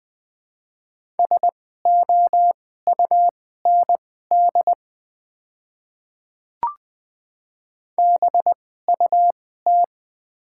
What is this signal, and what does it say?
Morse code, heard as a keyed tone, about 20 wpm, spaced as for 14 wpm, 700 Hz. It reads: SOUND BUT
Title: 2x Character Spacing - Top 100 Words - 20wpm